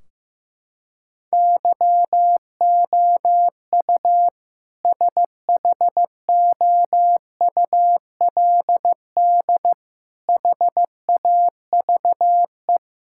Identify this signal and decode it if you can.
Morse code, heard as a keyed tone, about 15 wpm, 700 Hz. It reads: YOU SHOULD HAVE
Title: Sentences from Top 1600 Words 15wpm